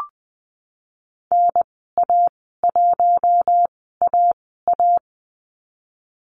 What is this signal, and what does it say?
Morse code, heard as a keyed tone, about 20 wpm, spaced as for 14 wpm, 700 Hz. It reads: NA1AA
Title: 055 Lesson 10g - Letter r - Call Signs 20wpm